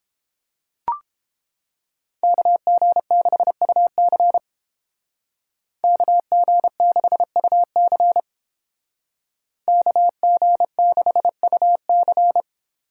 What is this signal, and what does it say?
Morse code, heard as a keyed tone, about 28 wpm, 700 Hz. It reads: KG6UC KG6UC KG6UC